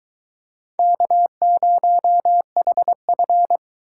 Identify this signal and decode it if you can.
Morse code, heard as a keyed tone, about 23 wpm, 700 Hz. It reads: K0HF